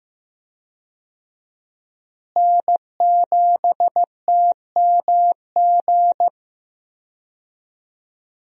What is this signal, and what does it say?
Morse code, heard as a keyed tone, about 15 wpm, 700 Hz. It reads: N7TMG
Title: USA Call Signs - 5 Characters - Repeated 3x - 15wpm